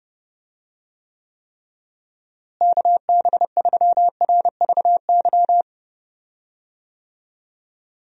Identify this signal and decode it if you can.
Morse code, heard as a keyed tone, about 32 wpm, 700 Hz. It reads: KB3RVY
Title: USA Call Signs - 6 Characters - Repeated 3x - 30wpm